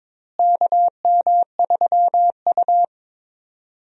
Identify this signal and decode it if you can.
Morse code, heard as a keyed tone, about 22 wpm, 700 Hz. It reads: KM3U
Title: USA Call Signs - 4 Characters - Repeated 2x - 22wpm